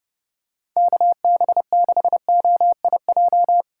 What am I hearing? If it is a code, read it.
Morse code, heard as a keyed tone, about 30 wpm, 700 Hz. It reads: KB6OIJ
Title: USA Call Signs - 6 Characters - 3x Character Spacing - 30wpm